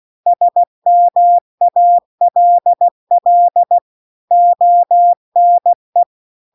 Morse code, heard as a keyed tone, about 16 wpm, 700 Hz. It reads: SMALL ONE
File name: CW-The-Big-Twoheaded-River-2-16WPM